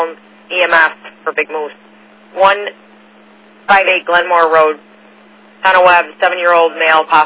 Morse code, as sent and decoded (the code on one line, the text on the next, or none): none